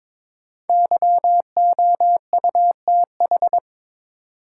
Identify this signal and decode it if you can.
Morse code, heard as a keyed tone, about 21 wpm, 700 Hz. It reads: YOUTH
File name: 5 Letter Words - First 3 Letters Given - 22wpm